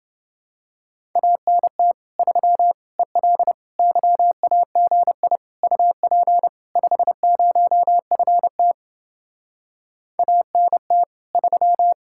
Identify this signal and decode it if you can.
Morse code, heard as a keyed tone, about 30 wpm, 700 Hz. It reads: ANT 3 EL YAGI UP 50FT ANT 3